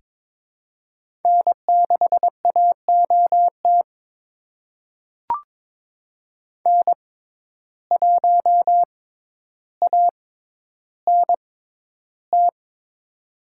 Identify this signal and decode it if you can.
Morse code, heard as a keyed tone, about 22 wpm, spaced as for 11 wpm, 700 Hz. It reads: N6AOT N1ANT
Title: USA Call Signs - 5 Characters - 6x Character Spacing - 22wpm